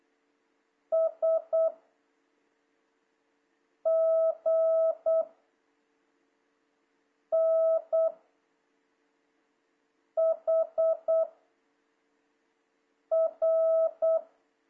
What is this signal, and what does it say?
Morse code, heard as a keyed tone, about 8 wpm, 650 Hz. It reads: S G N H R
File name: jh8cbh / 2026cw